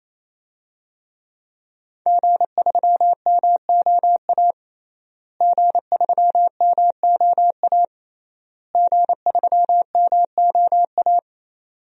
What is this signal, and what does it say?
Morse code, heard as a keyed tone, about 28 wpm, 700 Hz. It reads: G3MOA G3MOA G3MOA
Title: DXCC-100 Call Signs - 5 Characters - Repeated 3x - 28wpm